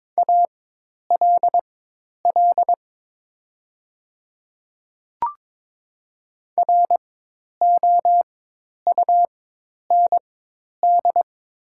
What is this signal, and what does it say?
Morse code, heard as a keyed tone, about 22 wpm, spaced as for 10 wpm, 700 Hz. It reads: ALL ROUND
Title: 4x Character Spacing - Top 200 Words - 22wpm